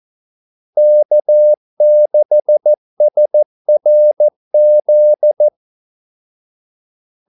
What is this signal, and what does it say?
Morse code, heard as a keyed tone, about 14 wpm, 600 Hz. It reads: K6SRZ
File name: Callsigns-14wpm